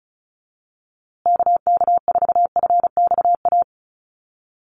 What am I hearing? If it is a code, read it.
Morse code, heard as a keyed tone, about 34 wpm, 700 Hz. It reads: KK4FXA
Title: USA Call Signs Repeated 2x 35wpm